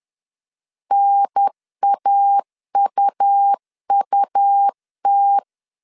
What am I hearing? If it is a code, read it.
Morse code, heard as a keyed tone, about 10 wpm, 800 Hz. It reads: NAUUT
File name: CW / DET / 1 3 1500C S C 10wpm 3335